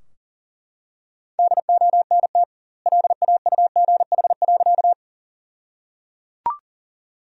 Morse code, heard as a keyed tone, about 37 wpm, 700 Hz. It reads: DONT LAUGH.
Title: 2x Word-Spacing - Sentences from Top 500 Words - 40wpm